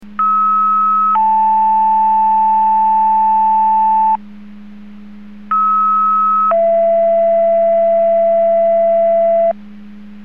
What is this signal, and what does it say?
Two-tone page, heard as a one-second tone, then a three-second tone, about 1290 then 880 Hz, about 1290 then 670 Hz. It is heard from 0.2 to 9.5 s.
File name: Portage 154.385 Paging